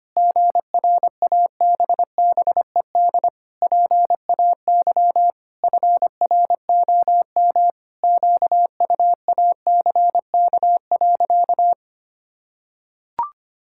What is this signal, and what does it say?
Morse code, heard as a keyed tone, about 25 wpm, 700 Hz. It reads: GRABBED PAY FROM QUACK.
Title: Pangram Challenge v2 - 25wpm